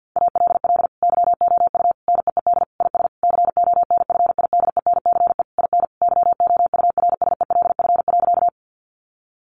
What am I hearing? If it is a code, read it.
Morse code, heard as a keyed tone, about 50 wpm, 700 Hz. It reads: ALL YOU NEED IS CONFIDENCE IN YOURSELF.
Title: Sentences from Top 1500 Words 50wpm